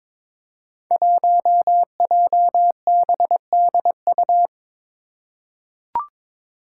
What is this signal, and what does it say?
Morse code, heard as a keyed tone, about 22 wpm, 700 Hz. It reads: 1JBDU
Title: Five Challenging Letter-Number Combinations 22wpm